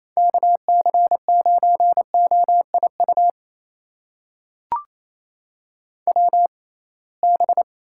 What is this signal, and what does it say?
Morse code, heard as a keyed tone, about 28 wpm, spaced as for 21 wpm, 700 Hz. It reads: KC9OIU WB
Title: USA Call Signs - 6 Characters - 6x Character Spacing - 28wpm